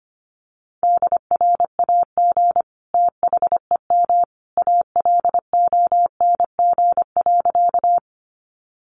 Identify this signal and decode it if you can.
Morse code, heard as a keyed tone, about 25 wpm, 700 Hz. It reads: DRAG THEM ALONG.
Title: Sentences from Top 1500 Words 25wpm